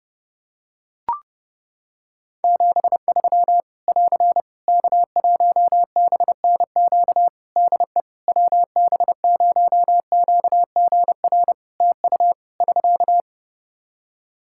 Morse code, heard as a keyed tone, about 29 wpm, 700 Hz. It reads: 73 <AR> K1BNQ DE WB0QGR TU <SK>
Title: Practice QSO - Exchange 4 - Ending - 30wpm